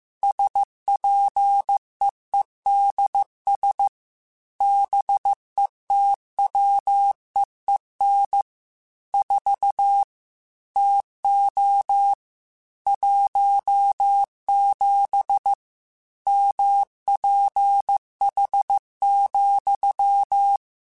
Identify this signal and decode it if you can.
Morse code, heard as a keyed tone, about 15 wpm, 800 Hz. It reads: SPEEDS BETWEEN 4 TO 17 MPH,